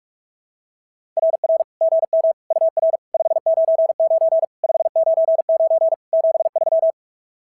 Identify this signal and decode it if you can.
Morse code, heard as a keyed tone, about 44 wpm, 660 Hz. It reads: RR GM UR 599 599 73
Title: SOTA QSO Practice 45wpm